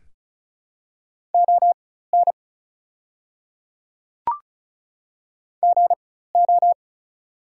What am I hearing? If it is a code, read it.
Morse code, heard as a keyed tone, about 36 wpm, spaced as for 16 wpm, 700 Hz. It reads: ON GO
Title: 4x Character Spacing - Top 100 Words - 35wpm